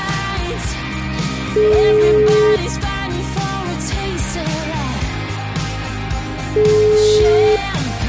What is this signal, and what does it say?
Ringback tone (Europe, 425 Hz), with noise.